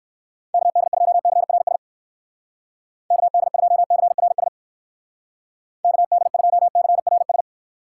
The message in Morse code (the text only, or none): KD2CRS KD2CRS KD2CRS